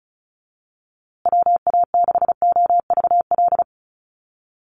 Morse code, heard as a keyed tone, about 34 wpm, 700 Hz. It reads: WA6OVL
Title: USA Call Signs Repeated 2x 35wpm